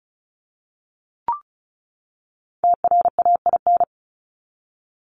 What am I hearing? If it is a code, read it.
Morse code, heard as a keyed tone, about 34 wpm, 700 Hz. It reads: TRAIN
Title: Top 1751-2000 Words 35wpm